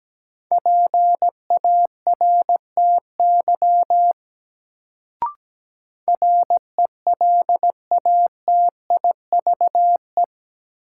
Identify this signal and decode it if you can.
Morse code, heard as a keyed tone, about 17 wpm, 700 Hz. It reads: PARTY RELATIVE
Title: Sentences from Top 1400 Words - New Word Drill Alt - 17wpm